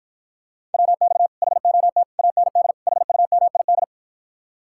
Morse code, heard as a keyed tone, about 53 wpm, 700 Hz. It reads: WX HOT AND HUMID